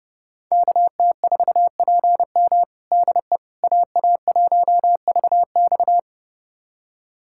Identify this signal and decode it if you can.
Morse code, heard as a keyed tone, about 30 wpm, 700 Hz. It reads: KT4PM DE AA1VX
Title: USA Callsign DE Callsign - 5 Characters - 30wpm